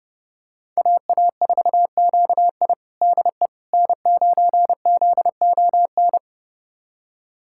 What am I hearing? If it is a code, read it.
Morse code, heard as a keyed tone, about 29 wpm, 700 Hz. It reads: AA4QI DE N9ZON
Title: USA Callsign DE Callsign - 5 Characters - 30wpm